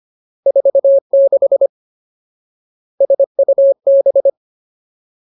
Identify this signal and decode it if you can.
Morse code, heard as a keyed tone, about 25 wpm, 550 Hz. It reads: 46 SUB
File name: Ultimate Word List / 25WPM Ultimate